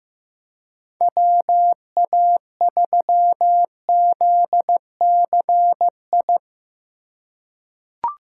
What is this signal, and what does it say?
Morse code, heard as a keyed tone, about 15 wpm, 700 Hz. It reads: WA3ZCI
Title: USA Call Signs - 6 Characters - 6x Character Spacing - 15wpm